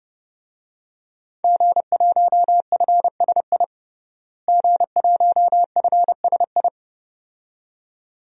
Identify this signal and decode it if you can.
Morse code, heard as a keyed tone, about 30 wpm, 700 Hz. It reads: G1FSI G1FSI
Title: DXCC-100 Call Signs - 5 Characters - Repeated 2x - 30wpm